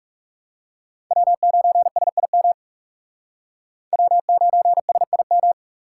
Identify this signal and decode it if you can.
Morse code, heard as a keyed tone, about 37 wpm, 700 Hz. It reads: W9SIM W9SIM
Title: Speed Racing - USA 1x3 Call Signs 30wpm